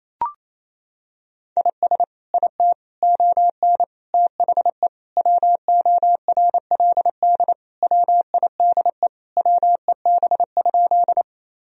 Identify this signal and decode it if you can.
Morse code, heard as a keyed tone, about 28 wpm, 700 Hz. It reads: IS IT ON THE WORLD WIDE WEB?
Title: Sentences from Top 1000 Words - Review Part 4 - 28wpm